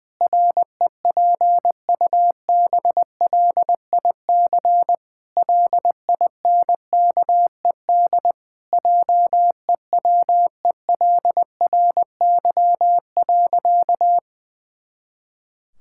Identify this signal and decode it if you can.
Morse code, heard as a keyed tone, about 21 wpm, 700 Hz. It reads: REPUBLIC LINKED JEWELRY.
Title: Pangram Challenge v2 - 20wpm